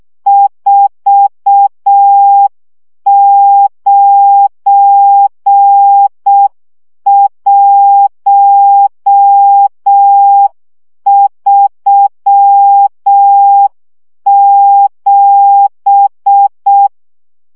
Morse code, heard as a keyed tone, about 6 wpm, 800 Hz. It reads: 49137
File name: morze vizsga 2011-2 5